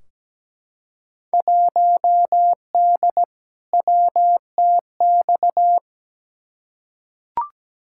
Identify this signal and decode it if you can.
Morse code, heard as a keyed tone, about 17 wpm, 700 Hz. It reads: 1D WTX
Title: Field Day Exchanges - 17wpm